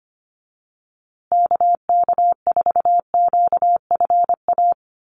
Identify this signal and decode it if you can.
Morse code, heard as a keyed tone, about 24 wpm, 700 Hz. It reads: KK4QFA
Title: USA Call Signs Repeated 3x 25wpm